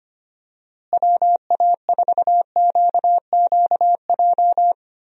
Morse code, heard as a keyed tone, about 25 wpm, 700 Hz. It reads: WA4QQJ
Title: USA Call Signs - 6 Characters - Repeated 2x - 25wpm